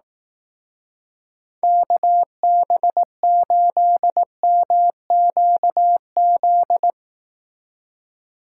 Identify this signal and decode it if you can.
Morse code, heard as a keyed tone, about 18 wpm, 700 Hz. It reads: KB8MQZ
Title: Speed Racing - USA 2x3 Call Signs 15wpm